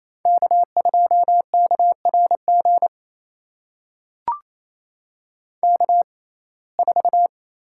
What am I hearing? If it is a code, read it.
Morse code, heard as a keyed tone, about 28 wpm, spaced as for 20 wpm, 700 Hz. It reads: K2KRG K4
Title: USA Call Signs - 5 Characters - 6x Character Spacing - 28wpm